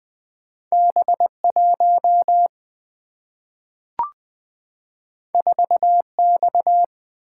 Morse code, heard as a keyed tone, about 20 wpm, 700 Hz. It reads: B 1 4 X
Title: Single-Letter Single-Number 20wpm